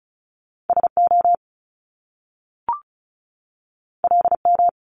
Morse code, heard as a keyed tone, about 36 wpm, 700 Hz. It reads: SO LM
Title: Top Two Letter Combinations in Words 35wpm